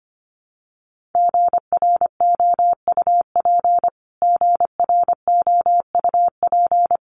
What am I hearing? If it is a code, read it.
Morse code, heard as a keyed tone, about 25 wpm, 700 Hz. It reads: GROUP GROUP
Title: Top 500 Words Repeated 25wpm